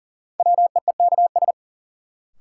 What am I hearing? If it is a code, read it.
Morse code, heard as a keyed tone, about 37 wpm, 700 Hz. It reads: WEEKS
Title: Sets of 5 Words - New Word Drill - 40wpm